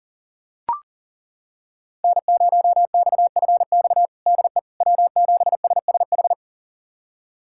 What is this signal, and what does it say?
Morse code, heard as a keyed tone, about 40 wpm, 700 Hz. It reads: N0XFX DE W7SSH
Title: USA Callsign DE Callsign - 5 Characters - 40wpm